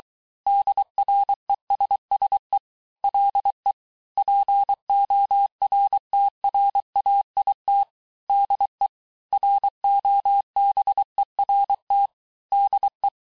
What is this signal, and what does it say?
Morse code, heard as a keyed tone, about 23 wpm, 800 Hz. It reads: DRESSE LE PORTRAIT DE ROBERT DE
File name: cours-du-lundi files / cours-1594--teste-clair-vitesse-17mpm-puis-22mpm